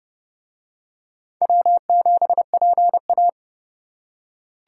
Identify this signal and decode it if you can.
Morse code, heard as a keyed tone, about 30 wpm, 700 Hz. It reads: W7PA